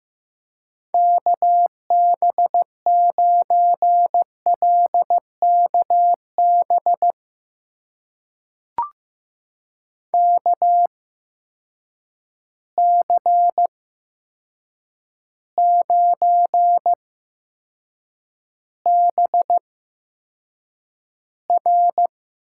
Morse code, heard as a keyed tone, about 15 wpm, spaced as for 7 wpm, 700 Hz. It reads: KB9LKB KC9BR
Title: USA Call Signs - 6 Characters - 8x Character Spacing - 15wpm